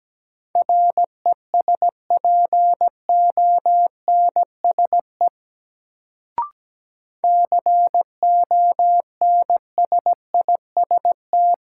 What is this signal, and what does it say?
Morse code, heard as a keyed tone, about 17 wpm, 700 Hz. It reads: RESPONSE CONSIST
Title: Sentences from Top 1400 Words - New Word Drill - 17wpm